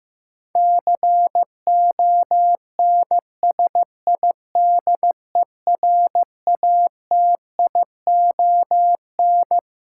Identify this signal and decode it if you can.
Morse code, heard as a keyed tone, about 15 wpm, 700 Hz. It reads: CONSIDERATION
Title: Sentences from Top 1500 Words - New Word Drill - 15wpm